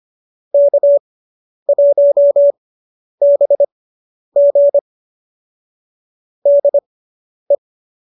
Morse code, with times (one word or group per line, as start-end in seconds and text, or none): 0.5-4.8 K1BG
6.5-7.6 DE